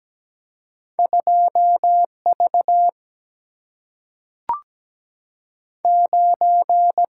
Morse code, heard as a keyed tone, about 17 wpm, 700 Hz. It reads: 2V 9
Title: Single-Letter Single-Number 17wpm